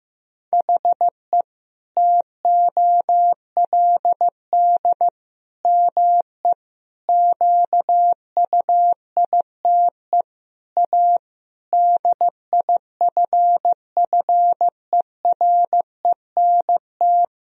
Morse code, heard as a keyed tone, about 15 wpm, 700 Hz. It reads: HE TOLD ME QUITE A DIFFERENT
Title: Sentences from Top 1000 Words - Review Part 3 - 15wpm